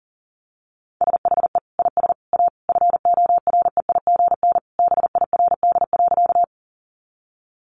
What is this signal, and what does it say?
Morse code, heard as a keyed tone, about 40 wpm, 700 Hz. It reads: SHE IS A FOREIGN BIRD.
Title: Sentences from Top 1400 Words 40wpm